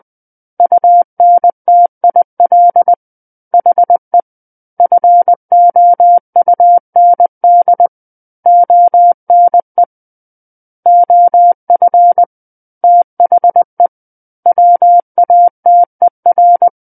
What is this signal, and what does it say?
Morse code, heard as a keyed tone, about 20 wpm, 700 Hz. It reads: UNTIL HE FOUND ONE OF THE WATER
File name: 20 WPM